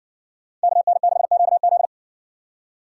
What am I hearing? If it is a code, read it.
Morse code, heard as a keyed tone, about 54 wpm, 700 Hz. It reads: KN6QZ